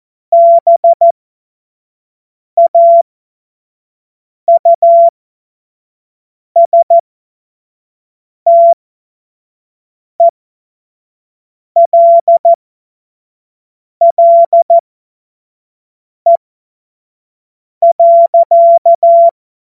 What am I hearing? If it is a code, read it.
Morse code, heard as a keyed tone, about 14 wpm, spaced as for 5 wpm, 680 Hz. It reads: BAUSTELLE.
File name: Morsekurs / Witze / Witz 02 Januar 2024 25 F70